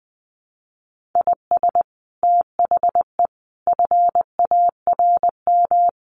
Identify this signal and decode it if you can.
Morse code, heard as a keyed tone, about 19 wpm, 700 Hz. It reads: IS THE FARM
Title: Sentences from Top 1400 Words 20wpm